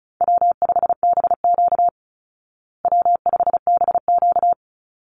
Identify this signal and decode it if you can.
Morse code, heard as a keyed tone, about 34 wpm, 700 Hz. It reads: W5BQ W5BQ
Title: Easy Call Signs Repeated 3x - 35wpm